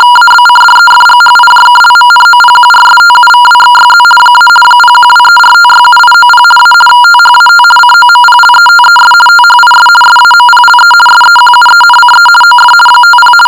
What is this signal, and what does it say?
RTTY, about 50 baud. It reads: ABCDEFGHIJKLMNOPQRSTUVWXYZ - 01234567890 A THAT COMPLETES THIS SAMPLE SEQUENCE OF 50
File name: shortwave audio / RTTY 50 baud sample